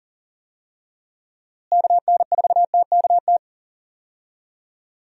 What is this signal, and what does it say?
Morse code, heard as a keyed tone, about 40 wpm, 700 Hz. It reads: KN4TKT
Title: USA Call Signs - 6 Characters - Repeated 3x - 40wpm